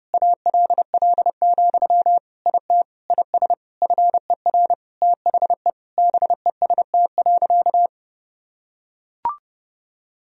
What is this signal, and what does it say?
Morse code, heard as a keyed tone, about 30 wpm, 700 Hz. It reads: ALL, IT IS FER THE BEST.
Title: Sentences from Top 500 Words and Abbreviations 30wpm